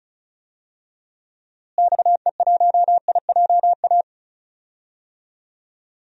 Morse code, heard as a keyed tone, about 36 wpm, 700 Hz. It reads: XE1IJA
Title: DXCC-100 Call Signs - 6 Characters - Repeated 2x - 35wpm